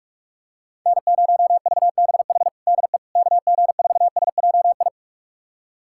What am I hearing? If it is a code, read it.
Morse code, heard as a keyed tone, about 42 wpm, 700 Hz. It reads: N0VBH DE KG4SJI
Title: USA Callsign DE Callsign 45wpm